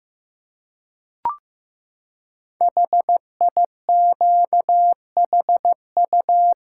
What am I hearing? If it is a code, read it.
Morse code, heard as a keyed tone, about 15 wpm, 700 Hz. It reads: HIQHU
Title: Code Groups - 5 Letters - 15wpm